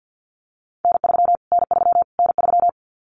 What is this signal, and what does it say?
Morse code, heard as a keyed tone, about 50 wpm, 700 Hz. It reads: N3 N3 N3
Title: Callsign Prefixes for Top 25 DXCC - Repeated 3x - 50wpm